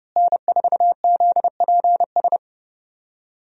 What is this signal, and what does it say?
Morse code, heard as a keyed tone, about 32 wpm, 700 Hz. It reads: N4ZPS
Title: USA Call Signs - 5 Characters - Repeated 3x - 30wpm